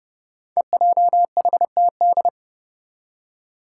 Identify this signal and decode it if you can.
Morse code, heard as a keyed tone, about 30 wpm, 700 Hz. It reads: EJHTD
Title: Five Challenging Letter-Number Combinations 30wpm